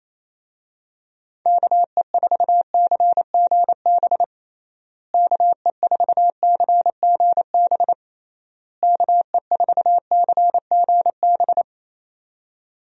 Morse code, heard as a keyed tone, about 28 wpm, 700 Hz. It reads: KE4CGB KE4CGB KE4CGB
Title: USA Call Signs - 6 Characters - Repeated 3x - 28wpm